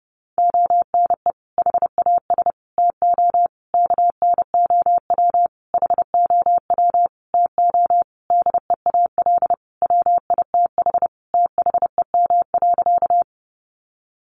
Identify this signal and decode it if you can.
Morse code, heard as a keyed tone, about 30 wpm, 700 Hz. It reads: ONE HAS TO KNOW HOW TO DEAL WITH THEM.
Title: Sentences from Top 800 Words 30wpm